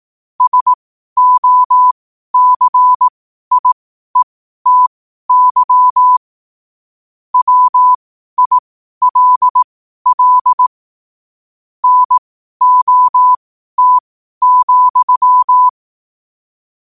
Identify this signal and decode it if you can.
Morse code, heard as a keyed tone, about 18 wpm, spaced as for 12 wpm, 1000 Hz. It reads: SOCIETY WILL NOT,
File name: qotdfiles / 12wpm